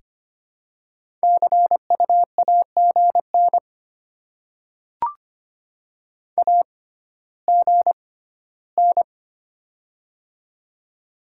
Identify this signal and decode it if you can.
Morse code, heard as a keyed tone, about 25 wpm, spaced as for 14 wpm, 700 Hz. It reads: CU AGN AGN
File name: Top 20 QSO Elements - 6x Character Spacing - 25wpm